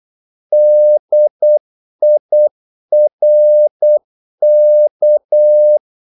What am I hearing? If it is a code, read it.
Morse code, heard as a keyed tone, about 8 wpm, 600 Hz. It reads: DIRK
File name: Korte ord DK 8wpm